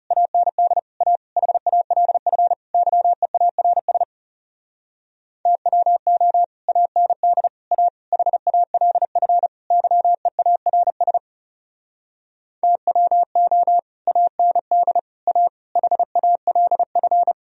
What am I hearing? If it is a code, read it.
Morse code, heard as a keyed tone, about 38 wpm, 700 Hz. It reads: AND A HALF YEARS TWO AND A HALF YEARS TWO AND A HALF